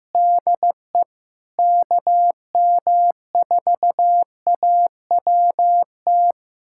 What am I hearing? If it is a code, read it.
Morse code, heard as a keyed tone, about 16 wpm, 700 Hz. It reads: DE KM4AWT
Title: USA Callsign DE Callsign - 6 Characters - 15wpm